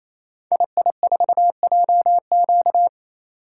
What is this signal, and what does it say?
Morse code, heard as a keyed tone, about 28 wpm, 700 Hz. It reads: II4JQ
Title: Code Groups - 5 Letter:Numbers - 28wpm